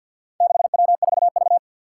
Morse code, heard as a keyed tone, about 51 wpm, 700 Hz. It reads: 6W4V
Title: Four Challenging Letter-Number Combinations 50wpm